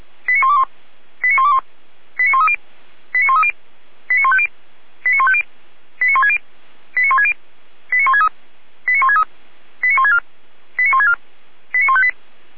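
Five-tone selective call (ZVEI), heard as quick runs of five tones, tones of about 70 ms. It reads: 89121 89121 89130 89130 89150 89150 89160 89160 89163 89163 89164 89164 89158
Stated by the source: ZVEI five-tone selective call 89140 received via SDR, fire brigade alerting from Feuerwehr Orlamünde.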